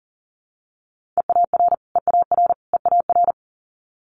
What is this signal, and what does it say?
Morse code, heard as a keyed tone, about 41 wpm, 700 Hz. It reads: EAR EAR EAR
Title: Multi-letter Phonemes Repeated 3x 40wpm